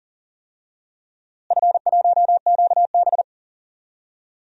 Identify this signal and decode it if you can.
Morse code, heard as a keyed tone, about 40 wpm, 700 Hz. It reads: F1QB
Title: DXCC-100 Call Signs - 4 Characters - Repeated 2x - 40wpm